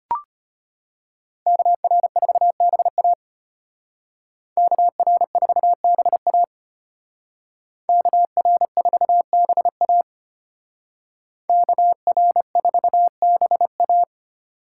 Morse code, heard as a keyed tone, about 32 wpm, 700 Hz. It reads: KR4BA KR4BA KR4BA KR4BA